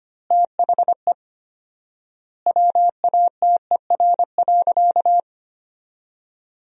Morse code, heard as a keyed tone, about 25 wpm, 700 Hz. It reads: THE WATER.
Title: 4x Word-Spacing - Sentences from Top 500 Words - 25wpm